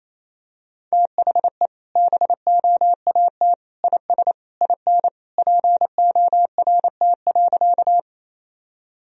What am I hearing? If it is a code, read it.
Morse code, heard as a keyed tone, about 28 wpm, 700 Hz. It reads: THE BOAT IS IN PORT.
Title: Sentences from Top 1000 Words - Review Part 1 - 28wpm